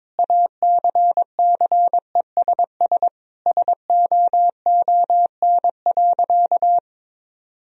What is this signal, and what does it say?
Morse code, heard as a keyed tone, about 21 wpm, 700 Hz. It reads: ACCESS SOON.
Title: Sentences from Top 1000 Words - Review Part 4 - 22wpm